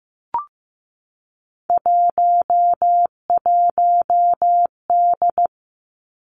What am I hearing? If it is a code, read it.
Morse code, heard as a keyed tone, about 15 wpm, 700 Hz. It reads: 11D
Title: Three Challenging Letter-Number Combinations 15wpm